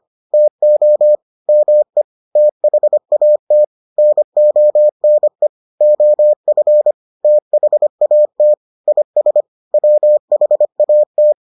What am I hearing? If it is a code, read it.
Morse code, heard as a keyed tone, about 25 wpm, 600 Hz. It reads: TO ME THAT NONE OF THAT IS WHAT